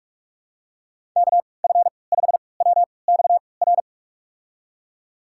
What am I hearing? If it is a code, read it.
Morse code, heard as a keyed tone, about 46 wpm, spaced as for 26 wpm, 700 Hz. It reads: KF5WXR
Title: USA Call Signs - 3x Character Spacing - 45wpm